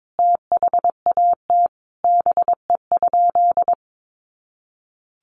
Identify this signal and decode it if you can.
Morse code, heard as a keyed tone, about 22 wpm, 700 Hz. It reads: THAT BE?
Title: Sentences from Top 100 Words 22wpm